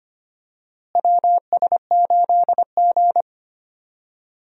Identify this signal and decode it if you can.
Morse code, heard as a keyed tone, about 25 wpm, 700 Hz. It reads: WS8G
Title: USA Call Signs - 4 Characters - Repeated 3x - 25wpm